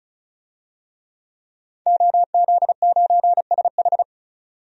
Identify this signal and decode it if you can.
Morse code, heard as a keyed tone, about 36 wpm, 700 Hz. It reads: OZ9SH
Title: DXCC-100 Call Signs - 5 Characters - 35wpm